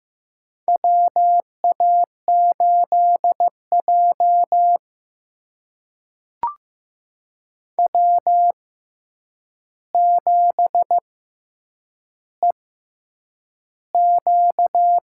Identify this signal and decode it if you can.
Morse code, heard as a keyed tone, about 15 wpm, spaced as for 8 wpm, 700 Hz. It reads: WA8J W7EQ
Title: USA Call Signs - 4 Characters - 6x Character Spacing - 15wpm